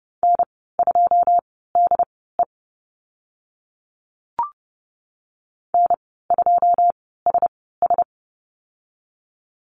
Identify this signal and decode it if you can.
Morse code, heard as a keyed tone, about 30 wpm, spaced as for 17 wpm, 700 Hz. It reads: N2DE N2SS
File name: Lesson 14g - Number 2 - Call Signs 30wpm